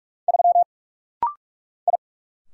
Morse code, heard as a keyed tone, about 44 wpm, 700 Hz. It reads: 3 I
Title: Single Letter-Number - Rapid-Fire 45wpm